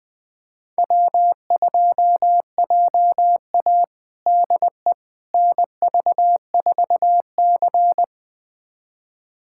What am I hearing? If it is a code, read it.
Morse code, heard as a keyed tone, about 21 wpm, 700 Hz. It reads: W2JA DE NV4C
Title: USA Callsign DE Callsign - 4 Characters - 20wpm